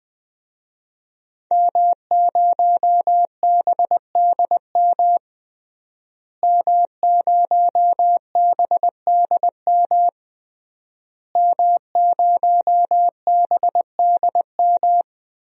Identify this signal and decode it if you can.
Morse code, heard as a keyed tone, about 20 wpm, 700 Hz. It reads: M0BDM M0BDM M0BDM
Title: DXCC-100 Call Signs - 5 Characters - Repeated 3x - 20wpm